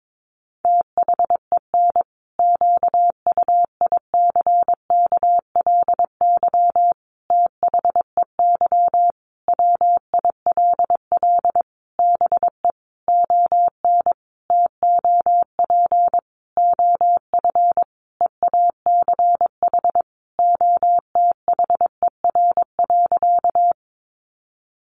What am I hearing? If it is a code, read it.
Morse code, heard as a keyed tone, about 22 wpm, 700 Hz. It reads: THEN QUICKLY THEY WILL BE ON TOP OF EACH OTHER.
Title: Sentences from Top 700 Words 22wpm